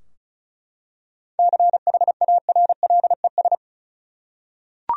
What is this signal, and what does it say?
Morse code, heard as a keyed tone, about 34 wpm, 700 Hz. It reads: CHARLES
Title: USA Common Names - 35wpm